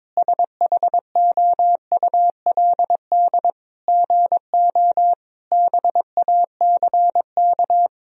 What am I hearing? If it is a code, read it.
Morse code, heard as a keyed tone, about 22 wpm, 700 Hz. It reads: SHOULD GO BACK
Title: Sentences from Top 1000 Words - Review Part 4 - 22wpm